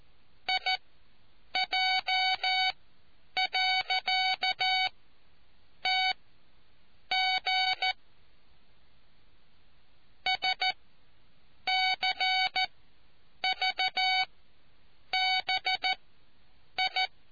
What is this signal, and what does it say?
Morse code, heard as a keyed tone, about 14 wpm, spaced as for 7 wpm, 750 Hz. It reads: IJ.TG SCVBI